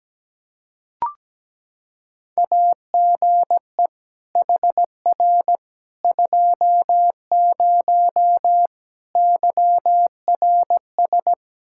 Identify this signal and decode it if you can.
Morse code, heard as a keyed tone, about 17 wpm, 700 Hz. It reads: AGE HR 20 YRS
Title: Practice QSO - Exchange 3 - More Info - 17wpm